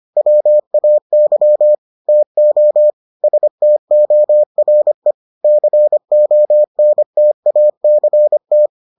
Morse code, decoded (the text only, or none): WAY TO STORE CONTACT